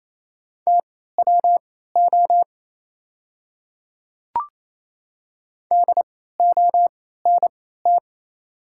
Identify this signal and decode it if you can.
Morse code, heard as a keyed tone, about 28 wpm, spaced as for 16 wpm, 700 Hz. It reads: TWO DONT
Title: 3x Character Spacing - Top 300 Words - 28wpm